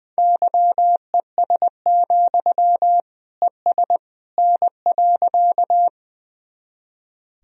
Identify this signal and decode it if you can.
Morse code, heard as a keyed tone, about 21 wpm, 700 Hz. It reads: YES, ES N.
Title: Sentences from Top 500 Words and Abbreviations 20wpm